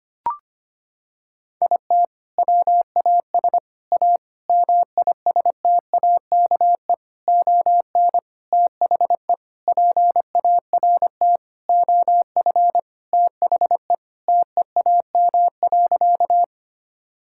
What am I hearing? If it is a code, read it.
Morse code, heard as a keyed tone, about 25 wpm, 700 Hz. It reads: IT WAS A MISTAKE ON THE PART OF THE TEAM.
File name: practice / 25wpm / Sentences with 4-Word Phrases - 25wpm